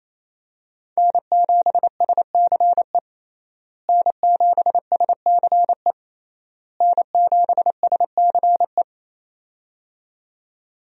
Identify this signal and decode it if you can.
Morse code, heard as a keyed tone, about 27 wpm, 700 Hz. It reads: N7SCE N7SCE N7SCE
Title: USA Call Signs - 5 Characters - Repeated 3x - 28wpm